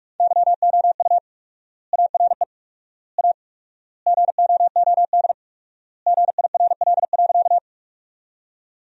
Morse code, heard as a keyed tone, about 41 wpm, 700 Hz. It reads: YOU ARE A GOOD GIRL.
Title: 4x Word-Spacing - Sentences from Top 400 Words - 45wpm